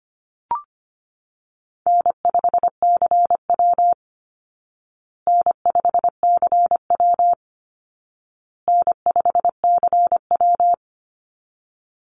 Morse code, heard as a keyed tone, about 25 wpm, 700 Hz. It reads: N5CW N5CW N5CW
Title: Easy Call Signs Repeated 3x - 25wpm